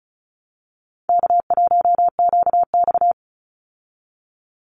Morse code, heard as a keyed tone, about 34 wpm, 700 Hz. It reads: K1QX
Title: Easy Call Signs Repeated 3x - 35wpm